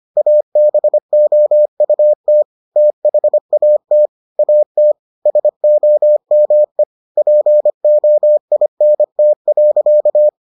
Morse code, heard as a keyed tone, about 25 wpm, 600 Hz. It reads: ABOUT THAT AT SOME POINT.